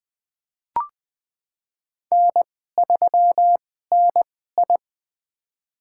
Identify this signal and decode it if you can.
Morse code, heard as a keyed tone, about 20 wpm, spaced as for 14 wpm, 700 Hz. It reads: N3NI
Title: USA Call Signs - 2x Character Spacing - 20wpm